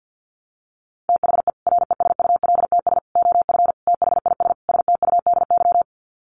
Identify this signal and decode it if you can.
Morse code, heard as a keyed tone, about 50 wpm, 700 Hz. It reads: THE RESULTS OF THIS STUDY
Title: Sets of 5 Words 50wpm